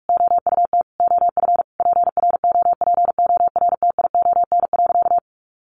Morse code, heard as a keyed tone, about 44 wpm, 700 Hz. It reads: OUT OF PROPORTION.
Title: A Sound Of Thunder 45wpm